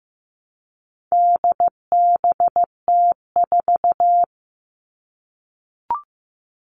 Morse code, heard as a keyed tone, about 15 wpm, 700 Hz. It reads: DBT4
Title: All Cut Numbers - 4-Digit 15wpm